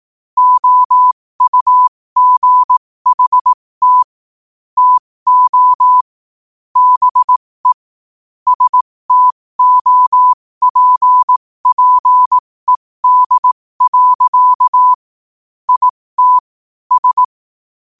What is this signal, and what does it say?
Morse code, heard as a keyed tone, about 18 wpm, spaced as for 15 wpm, 1000 Hz. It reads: OUGHT TO BE STOPPED. IT S
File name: qotdfiles / 15wpm